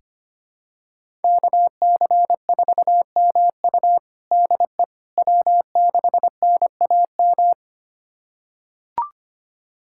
Morse code, heard as a keyed tone, about 25 wpm, 700 Hz. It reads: KC4MU DE W6NAM
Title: USA Callsign DE Callsign - 5 Characters - 25wpm